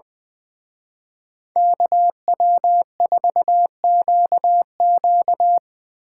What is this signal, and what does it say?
Morse code, heard as a keyed tone, about 24 wpm, 700 Hz. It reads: KW4QQ
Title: Speed Racing - USA Call Signs 20wpm